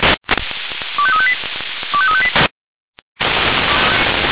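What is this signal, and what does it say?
Five-tone selective call (ZVEI), heard as quick runs of five tones, tones of about 70 ms. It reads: 25358 25358
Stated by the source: Five-tone ZVEI selective call received on an SDR.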